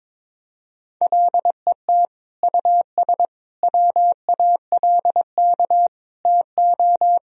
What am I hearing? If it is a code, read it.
Morse code, heard as a keyed tone, about 22 wpm, 700 Hz. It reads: LET US WALK TO
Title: Sentences from Top 1000 Words - Review Part 2 - 22wpm